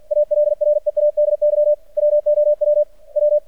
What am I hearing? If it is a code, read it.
Morse code, heard as a keyed tone, about 50 wpm, 600 Hz. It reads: ALREADY WOW W
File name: archived / CRSnet